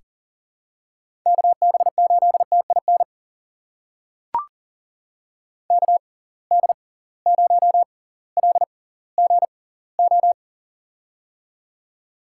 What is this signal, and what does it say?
Morse code, heard as a keyed tone, about 40 wpm, spaced as for 21 wpm, 700 Hz. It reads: KB8TIN KD0LGO
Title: USA Call Signs - 6 Characters - 6x Character Spacing - 40wpm